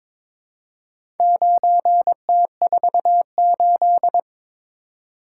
Morse code, heard as a keyed tone, about 22 wpm, 700 Hz. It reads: 9T48
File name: practice / 22wpm / Four Letter-Number Combinations - 22wpm